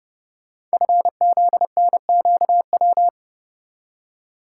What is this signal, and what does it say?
Morse code, heard as a keyed tone, about 30 wpm, 700 Hz. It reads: FZNQW
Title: Code Groups - 5 Letters - 30wpm